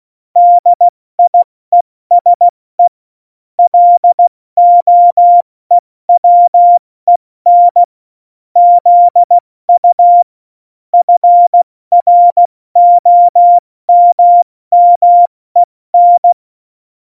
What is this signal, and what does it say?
Morse code, heard as a keyed tone, about 16 wpm, spaced as for 14 wpm, 700 Hz. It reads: DIESE LOEWEN ZU FROMMEN
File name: Morsekurs / Witze / Witz 09. Oktober 2018, 70 F80